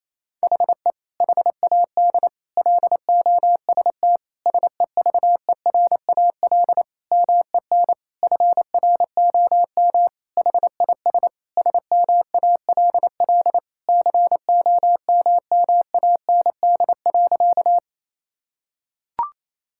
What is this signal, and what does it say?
Morse code, heard as a keyed tone, about 28 wpm, 700 Hz. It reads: HE HAD LOST SEVERAL MEN FROM HIS SMALL COMMAND.
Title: Sentences from Top 1600 Words 28wpm